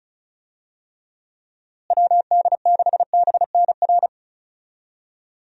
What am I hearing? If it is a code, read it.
Morse code, heard as a keyed tone, about 36 wpm, 700 Hz. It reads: WD6BNR
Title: USA Call Signs - 6 Characters - Repeated 2x - 35wpm